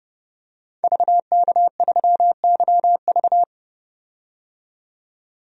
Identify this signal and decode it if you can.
Morse code, heard as a keyed tone, about 32 wpm, 700 Hz. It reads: VK3YV
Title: DXCC-100 Call Signs - 5 Characters - 30wpm